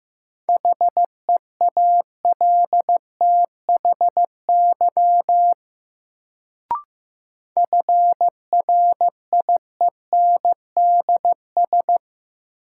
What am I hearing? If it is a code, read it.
Morse code, heard as a keyed tone, about 15 wpm, 700 Hz. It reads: HEALTHY FRIENDS
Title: Sentences from Top 1000 Words - Review Part 5 - New Word Drill - 15wpm